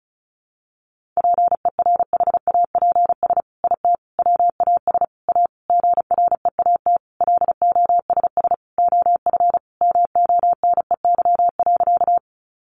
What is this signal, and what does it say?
Morse code, heard as a keyed tone, about 34 wpm, 700 Hz. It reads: PERHAPS IT WAS A GREAT LOSS OF MONEY.
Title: Sentences from Top 1400 Words 35wpm